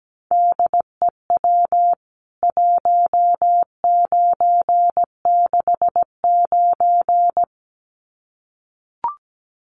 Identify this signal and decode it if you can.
Morse code, heard as a keyed tone, about 17 wpm, 700 Hz. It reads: DEW 1969
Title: CWT Exchange 17wpm